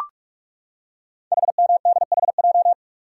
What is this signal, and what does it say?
Morse code, heard as a keyed tone, about 44 wpm, 690 Hz. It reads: HMDHJ
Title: Code Groups - 5 Letters - 45wpm